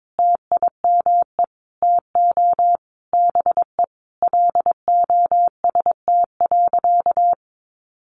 Morse code, heard as a keyed tone, about 22 wpm, 700 Hz. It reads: TIME TO BE LOST.
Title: Sentences from Top 1400 Words 22wpm